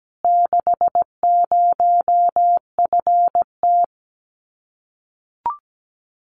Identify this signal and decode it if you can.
Morse code, heard as a keyed tone, about 17 wpm, 700 Hz. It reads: 60FT
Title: Common QSO Phrases Encore 17wpm